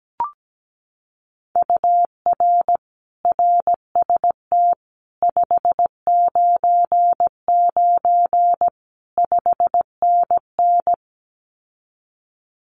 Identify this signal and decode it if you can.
Morse code, heard as a keyed tone, about 17 wpm, 700 Hz. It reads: UR RST 599 5NN
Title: Common QSO Phrases Encore 17wpm